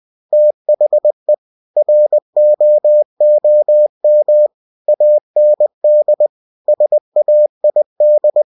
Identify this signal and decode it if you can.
Morse code, heard as a keyed tone, about 20 wpm, 600 Hz. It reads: THE ROOM AND SAID
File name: cw-books / winnie-the-pooh / winnie-20wpm-page-0063